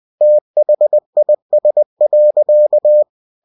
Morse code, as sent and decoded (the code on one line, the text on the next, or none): - .... .. ... .-.-.-
THIS.